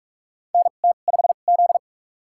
Morse code, heard as a keyed tone, about 44 wpm, spaced as for 32 wpm, 700 Hz. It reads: NT5Z
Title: USA Call Signs - 4 Characters - 2x Character Spacing - 45wpm